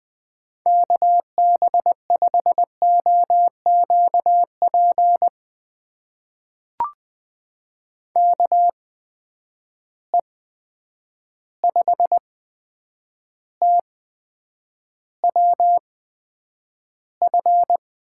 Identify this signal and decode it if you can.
Morse code, heard as a keyed tone, about 20 wpm, spaced as for 9 wpm, 700 Hz. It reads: KB5OQP KE5TWF
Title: USA Call Signs - 6 Characters - 8x Character Spacing - 20wpm